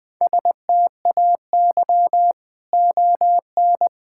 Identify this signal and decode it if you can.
Morse code, heard as a keyed tone, about 21 wpm, 700 Hz. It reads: STAY ON
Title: Sets of 6 Words - 20wpm